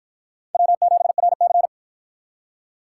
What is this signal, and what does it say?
Morse code, heard as a keyed tone, about 47 wpm, 700 Hz. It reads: W7RC